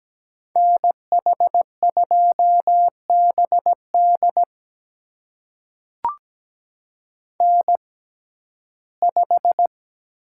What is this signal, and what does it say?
Morse code, heard as a keyed tone, about 17 wpm, spaced as for 12 wpm, 700 Hz. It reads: NH2BD N5
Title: USA Call Signs - 5 Characters - 6x Character Spacing - 17wpm